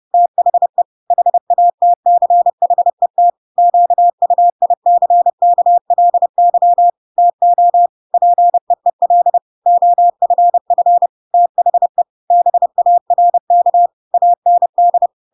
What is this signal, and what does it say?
Morse code, heard as a keyed tone, about 30 wpm, 700 Hz. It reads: THE HATCHET QUICKLY TO PEEL OFF THE BARK AND
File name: Pinocchio / Pinocchio ch1 30 30 ik1hge